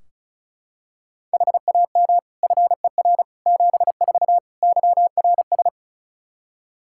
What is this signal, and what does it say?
Morse code, heard as a keyed tone, about 34 wpm, 700 Hz. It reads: HAM FER 74 YRS
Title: Practice QSO - Exchange 3 - More Info - 35wpm